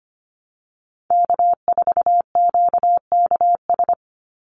Morse code, heard as a keyed tone, about 25 wpm, 690 Hz. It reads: K4QKS
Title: USA Call Signs 25wpm